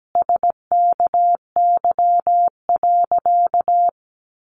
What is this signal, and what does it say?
Morse code, heard as a keyed tone, about 17 wpm, 700 Hz. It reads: SKY.
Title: Sentences from Top 1200 Words 17wpm